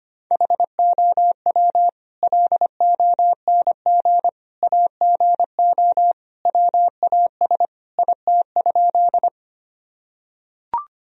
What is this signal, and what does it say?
Morse code, heard as a keyed tone, about 25 wpm, 700 Hz. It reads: HOW LONG AGO WAS IT?
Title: Sentences from Top 1000 Words - Review Part 3 - 25wpm